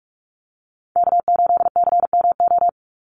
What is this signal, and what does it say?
Morse code, heard as a keyed tone, about 44 wpm, 700 Hz. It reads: K8CMO
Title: Call Signs - 45wpm